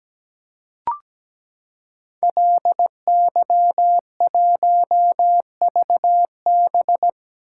Morse code, heard as a keyed tone, about 17 wpm, 700 Hz. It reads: LY1VB
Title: Code Groups - 5 Letter:Numbers - 17wpm